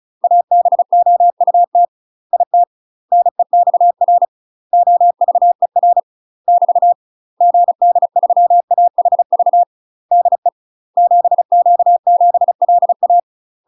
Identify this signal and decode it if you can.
Morse code, heard as a keyed tone, about 34 wpm, 700 Hz. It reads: ABOUT IT NEXR OVER <BT> GD3AHV DE 7Q7LA
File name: UK MASTER QSOs TRAINING/35WPM